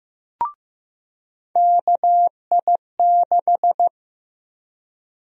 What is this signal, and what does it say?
Morse code, heard as a keyed tone, about 15 wpm, 700 Hz. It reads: KI6
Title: USA Call Sign Prefixes - 15wpm